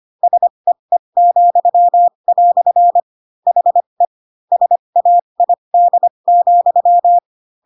Morse code, heard as a keyed tone, about 25 wpm, spaced as for 22 wpm, 700 Hz. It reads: SEE," HE SAID,